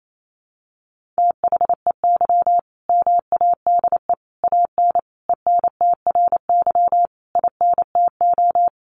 Morse code, heard as a keyed tone, about 28 wpm, 700 Hz. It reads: THEY MADE AN ENTRY INTO
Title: Sentences from Top 1500 Words 28wpm